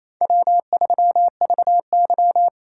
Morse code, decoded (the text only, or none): W3VY